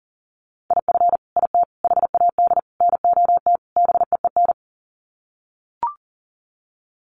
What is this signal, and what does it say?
Morse code, heard as a keyed tone, about 40 wpm, 700 Hz. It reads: IF IT HAD NOT BEEN
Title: Sets of 5 Words Encore 40wpm